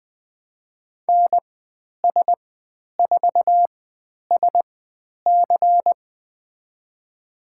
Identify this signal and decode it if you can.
Morse code, heard as a keyed tone, about 20 wpm, spaced as for 10 wpm, 700 Hz. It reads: NS4SC
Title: 095 Lesson 16g - Letter c - Call Signs 20wpm